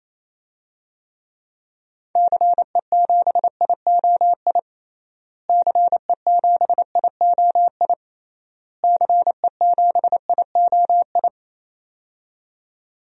Morse code, heard as a keyed tone, about 28 wpm, 700 Hz. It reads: CE7IOI CE7IOI CE7IOI
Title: DXCC-100 Call Signs - 6 Characters - Repeated 3x - 28wpm